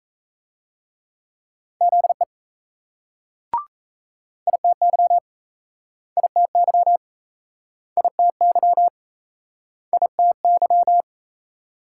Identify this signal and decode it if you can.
Morse code, heard as a keyed tone, about 38 wpm, 700 Hz. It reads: ZE ITY ITY ITY ITY